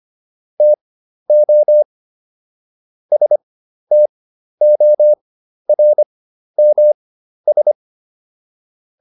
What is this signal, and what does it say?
Morse code, heard as a keyed tone, about 25 wpm, spaced as for 12 wpm, 600 Hz. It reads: TO STORMS